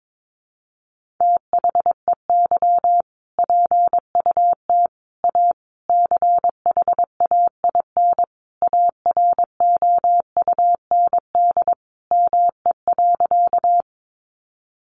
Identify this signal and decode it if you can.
Morse code, heard as a keyed tone, about 22 wpm, 700 Hz. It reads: THEY PUT A CHAIN AROUND ME.
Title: Sentences from Top 1500 Words 22wpm